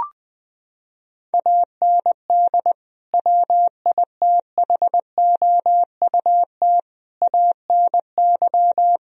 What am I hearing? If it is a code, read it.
Morse code, heard as a keyed tone, about 20 wpm, 700 Hz. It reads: AND WITHOUT ANY
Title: Sentences from Top 1000 Words - Review Part 4 - 20wpm